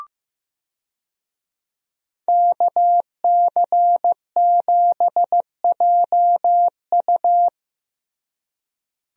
Morse code, heard as a keyed tone, about 15 wpm, 700 Hz. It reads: KC7JU
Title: USA Call Signs - 5 Characters - Repeated 2x - 15wpm